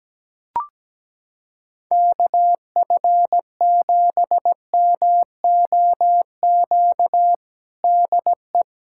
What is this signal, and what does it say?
Morse code, heard as a keyed tone, about 17 wpm, 700 Hz. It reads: KF7MOQ DE
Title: USA Callsign DE Callsign - 6 Characters - 17wpm